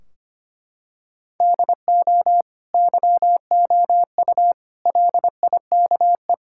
Morse code, heard as a keyed tone, about 25 wpm, 700 Hz. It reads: DO YOU LIKE
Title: Sentences from Top 1600 Words 25wpm